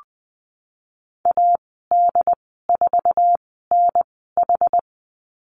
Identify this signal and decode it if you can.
Morse code, heard as a keyed tone, about 20 wpm, spaced as for 15 wpm, 700 Hz. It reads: AD4NH
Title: 088 Lesson 15g - Number 5 - Call Signs 20wpm